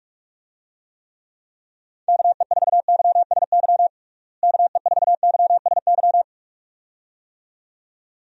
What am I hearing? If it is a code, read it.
Morse code, heard as a keyed tone, about 46 wpm, 700 Hz. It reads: KE4YSY KE4YSY
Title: USA Call Signs - 6 Characters - Repeated 2x - 45wpm